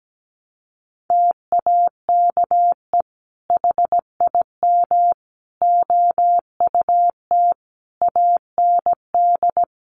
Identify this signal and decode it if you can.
Morse code, heard as a keyed tone, about 17 wpm, 700 Hz. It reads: TAKE HIM OUT AND
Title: Sentences from Top 1200 Words 17wpm